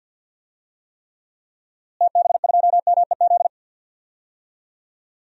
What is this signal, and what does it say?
Morse code, heard as a keyed tone, about 51 wpm, 700 Hz. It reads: TB2MEZ
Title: DXCC-100 Call Signs - 6 Characters - Repeated 3x - 50wpm